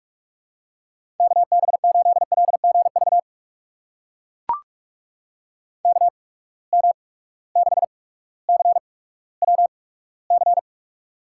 Morse code, heard as a keyed tone, about 44 wpm, spaced as for 19 wpm, 700 Hz. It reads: KB8LGV KM6CWC
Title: USA Call Signs - 6 Characters - 8x Character Spacing - 45wpm